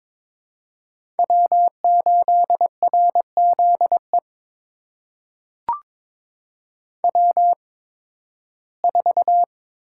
Morse code, heard as a keyed tone, about 22 wpm, spaced as for 14 wpm, 700 Hz. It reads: W8RZE W4
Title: USA Call Signs - 5 Characters - 8x Character Spacing - 22wpm